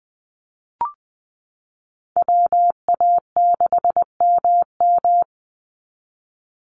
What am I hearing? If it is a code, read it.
Morse code, heard as a keyed tone, about 20 wpm, 700 Hz. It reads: WA6MM
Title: NA SOTA Chaser Callsigns 20wpm